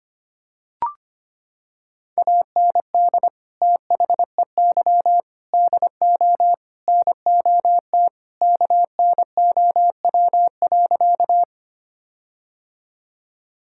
Morse code, heard as a keyed tone, about 25 wpm, 700 Hz. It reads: AND THEY DO NOT KNOW.
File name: practice / 25wpm / Sentences with 4-Word Phrases - 25wpm